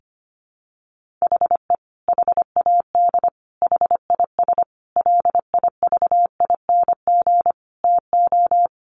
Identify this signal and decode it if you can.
Morse code, heard as a keyed tone, about 25 wpm, 700 Hz. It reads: HE HAD HIS LIVING TO
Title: Sentences from Top 1500 Words 25wpm